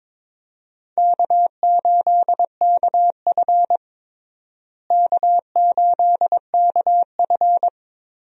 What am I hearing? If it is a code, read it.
Morse code, heard as a keyed tone, about 22 wpm, 700 Hz. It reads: K8KF K8KF
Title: USA Call Signs - 4 Characters - Repeated 2x - 22wpm